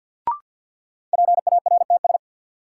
Morse code, heard as a keyed tone, about 52 wpm, 700 Hz. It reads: PARTS
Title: Sentences from Top 1000 Words - Review Part 6 - New Word Drill - 50wpm